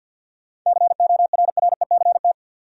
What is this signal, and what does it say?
Morse code, heard as a keyed tone, about 50 wpm, 700 Hz. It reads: CORRECT
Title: Sentences from Top 700 Words - New Word Drill - 50wpm